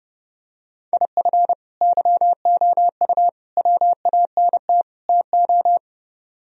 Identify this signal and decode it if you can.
Morse code, heard as a keyed tone, about 32 wpm, 700 Hz. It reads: IF YOU WANT TO